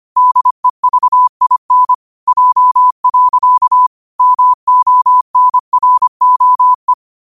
Morse code, heard as a keyed tone, about 25 wpm, 1000 Hz. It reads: DEVIN J. MONROE
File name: qotdfiles / 25wpm